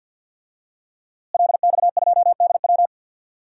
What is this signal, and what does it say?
Morse code, heard as a keyed tone, about 52 wpm, 700 Hz. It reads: LX2DW
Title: DXCC-100 Call Signs - 5 Characters - Repeated 2x - 50wpm